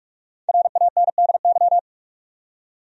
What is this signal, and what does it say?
Morse code, heard as a keyed tone, about 47 wpm, 700 Hz. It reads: RANDY
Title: USA Common Names - 45wpm